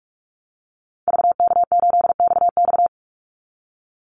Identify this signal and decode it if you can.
Morse code, heard as a keyed tone, about 44 wpm, 700 Hz. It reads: VK8XX
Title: Call Signs - 45wpm